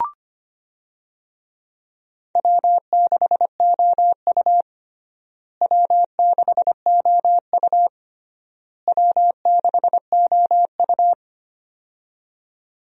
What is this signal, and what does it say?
Morse code, heard as a keyed tone, about 25 wpm, 700 Hz. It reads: W6OU W6OU W6OU
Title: USA Call Signs - 4 Characters - Repeated 3x - 25wpm